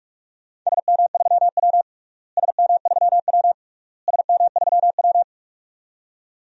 Morse code, heard as a keyed tone, about 47 wpm, 700 Hz. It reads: SM3W SM3W SM3W
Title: DXCC-100 Call Signs - 4 Characters - Repeated 3x - 45wpm